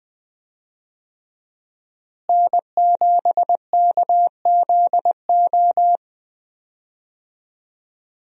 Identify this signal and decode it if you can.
Morse code, heard as a keyed tone, about 20 wpm, 700 Hz. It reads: N7KZO
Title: USA Call Signs - 5 Characters - Repeated 3x - 20wpm